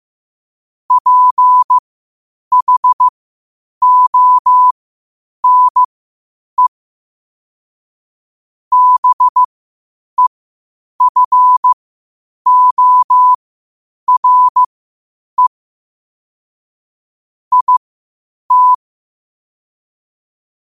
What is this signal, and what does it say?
Morse code, heard as a keyed tone, about 15 wpm, spaced as for 8 wpm, 1000 Hz. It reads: PHONE BEFORE IT
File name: qotdfiles / 8wpm